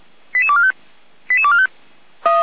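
Five-tone selective call (ZVEI), heard as quick runs of five tones, tones of about 70 ms. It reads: 88245 88245
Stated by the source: ZVEI five-tone selective call received on an SDR.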